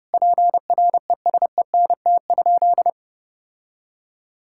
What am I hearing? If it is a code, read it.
Morse code, heard as a keyed tone, about 30 wpm, 700 Hz. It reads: PRESENT?
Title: Sentences from Top 1000 Words - Review Part 6 - 30wpm